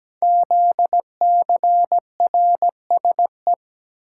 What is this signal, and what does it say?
Morse code, heard as a keyed tone, about 17 wpm, 700 Hz. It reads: ZCRSE
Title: Code Groups - 5 Letters - 17wpm